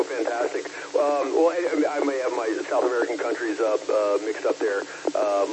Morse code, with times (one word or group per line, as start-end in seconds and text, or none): none